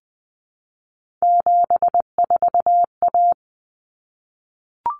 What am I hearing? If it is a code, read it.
Morse code, heard as a keyed tone, about 20 wpm, 700 Hz. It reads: 74A
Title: Common Cut Numbers - 3-Digit 20wpm